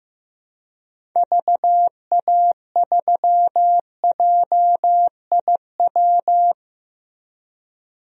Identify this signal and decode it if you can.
Morse code, heard as a keyed tone, about 15 wpm, 700 Hz. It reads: VA3JIW
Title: DXCC-100 Call Signs - 6 Characters - 15wpm